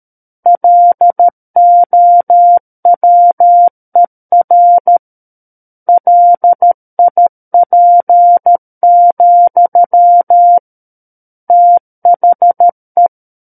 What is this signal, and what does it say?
Morse code, heard as a keyed tone, about 13 wpm, 700 Hz. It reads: LOWER LIP, THE
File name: Book One / 04-Chapter 4 - 13 WPM